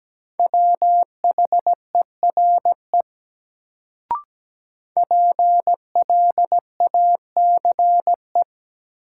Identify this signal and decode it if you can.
Morse code, heard as a keyed tone, about 17 wpm, 700 Hz. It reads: WHERE PLACE
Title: Sentences from Top 200 Words - New Word Drill - 17wpm